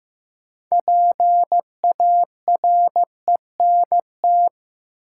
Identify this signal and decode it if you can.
Morse code, heard as a keyed tone, about 16 wpm, 700 Hz. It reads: PARENT
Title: Six Letter Words - 15wpm